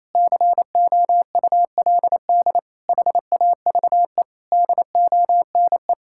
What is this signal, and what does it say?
Morse code, heard as a keyed tone, about 28 wpm, 700 Hz. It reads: COULD HAVE DONE
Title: Sentences from Top 1600 Words 28wpm